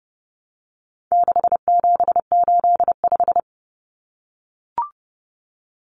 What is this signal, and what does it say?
Morse code, heard as a keyed tone, about 30 wpm, 700 Hz. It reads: 6785
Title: Common Cut Numbers - 4-Digit 30wpm